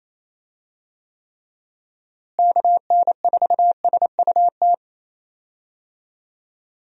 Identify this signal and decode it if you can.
Morse code, heard as a keyed tone, about 28 wpm, 700 Hz. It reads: KN4SUT
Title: USA Call Signs - 6 Characters - Repeated 2x - 28wpm